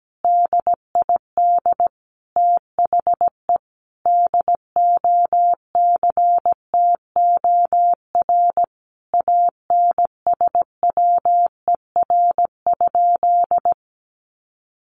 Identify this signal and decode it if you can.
Morse code, heard as a keyed tone, about 17 wpm, 700 Hz. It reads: DID THE DOCTOR ANSWER?
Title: Sentences from Top 1500 Words 17wpm